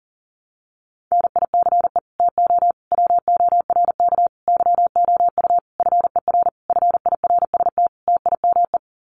Morse code, heard as a keyed tone, about 40 wpm, 700 Hz. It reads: NICE TO WORK YOU FER FIRST TIME
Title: Common QSO Phrases 40wpm